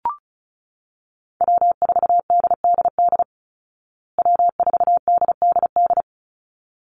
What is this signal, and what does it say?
Morse code, heard as a keyed tone, about 34 wpm, 700 Hz. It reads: W4DDD W4DDD
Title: Call Signs Repeated 3x - 35wpm